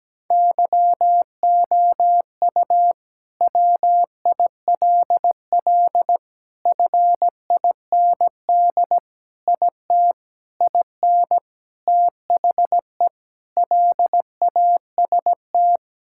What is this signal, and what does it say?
Morse code, heard as a keyed tone, about 17 wpm, 700 Hz. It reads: YOU WILL FIND IT IN THE LAST
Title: Sentences from Top 1000 Words - Review Part 5 - 17wpm